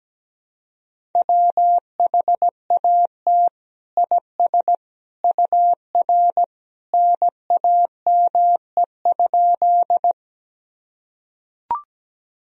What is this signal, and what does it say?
Morse code, heard as a keyed tone, about 17 wpm, 700 Hz. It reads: WHAT IS UR NAME?
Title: Sentences from Top 200 Words and Abbreviations 17wpm